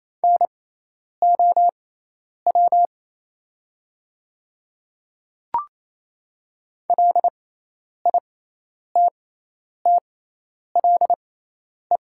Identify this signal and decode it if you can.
Morse code, heard as a keyed tone, about 28 wpm, spaced as for 10 wpm, 700 Hz. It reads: NOW LITTLE
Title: 6x Character Spacing - Top 200 Words - 28wpm